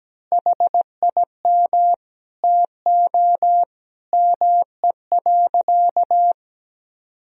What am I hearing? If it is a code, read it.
Morse code, heard as a keyed tone, about 17 wpm, 700 Hz. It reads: HIM TO ME.
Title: Sentences from Top 1000 Words - Review Part 3 - 17wpm